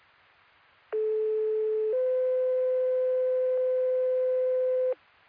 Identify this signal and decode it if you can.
Two-tone page, heard as a one-second tone, then a three-second tone, about 430 then 510 Hz. It is heard from 0.9 to 4.9 s.